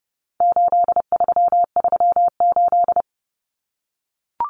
Morse code, heard as a keyed tone, about 30 wpm, 700 Hz. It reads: 8338
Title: Numbers - 4-Digit 30wpm